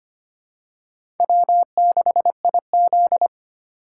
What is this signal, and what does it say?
Morse code, heard as a keyed tone, about 25 wpm, 700 Hz. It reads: W6IZ
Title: USA Call Signs - 4 Characters - 3x Character Spacing - 25wpm